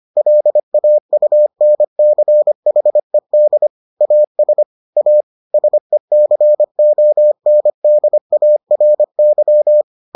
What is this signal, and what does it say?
Morse code, heard as a keyed tone, about 25 wpm, 600 Hz. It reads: LAUNCHED AS A SECONDARY